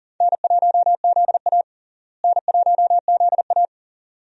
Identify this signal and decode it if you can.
Morse code, heard as a keyed tone, about 40 wpm, 700 Hz. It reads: N1ZA N1ZA
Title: USA Call Signs - 4 Characters - Repeated 2x - 40wpm